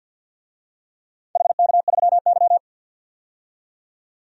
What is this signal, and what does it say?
Morse code, heard as a keyed tone, about 52 wpm, 700 Hz. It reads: HK3Y
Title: DXCC-100 Call Signs - 4 Characters - 50wpm